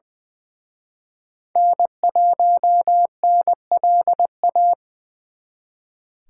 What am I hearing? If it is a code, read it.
Morse code, heard as a keyed tone, about 24 wpm, 700 Hz. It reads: N1NLA
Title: Speed Racing - USA 1x3 Call Signs 20wpm